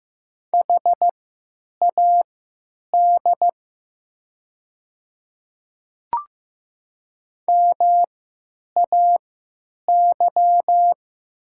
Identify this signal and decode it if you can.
Morse code, heard as a keyed tone, about 15 wpm, spaced as for 9 wpm, 700 Hz. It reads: HAD MAY
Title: 3x Character Spacing - Top 100 Words - 15wpm